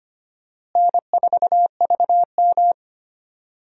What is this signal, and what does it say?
Morse code, heard as a keyed tone, about 25 wpm, 700 Hz. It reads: N4VM
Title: USA Call Signs - 4 Characters - Repeated 3x - 25wpm